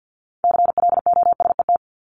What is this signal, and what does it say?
Morse code, heard as a keyed tone, about 50 wpm, 700 Hz. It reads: CLOSET
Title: Top 1751-2000 Words 50wpm